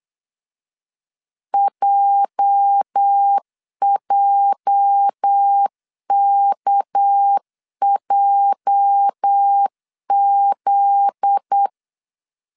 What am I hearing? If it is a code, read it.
Morse code, heard as a keyed tone, about 8 wpm, 800 Hz. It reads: JJKJZ
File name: LEZIONI / 8 YJ45 / 8 YJ 8wpm 3313